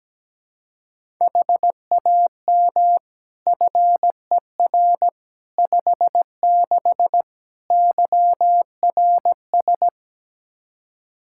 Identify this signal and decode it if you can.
Morse code, heard as a keyed tone, about 17 wpm, 700 Hz. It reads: HAM FER 56 YRS
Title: Practice QSO - Exchange 3 - More Info - 17wpm